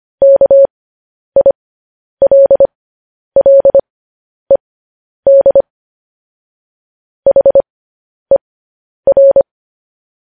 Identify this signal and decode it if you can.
Morse code, heard as a keyed tone, about 25 wpm, spaced as for 10 wpm, 550 Hz. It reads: KILLED HER
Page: news headlines in morse code at 10 wpm